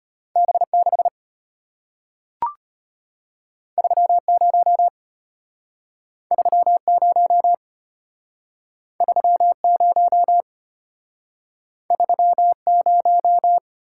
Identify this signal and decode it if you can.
Morse code, heard as a keyed tone, about 33 wpm, 700 Hz. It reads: B6 30 30 30 30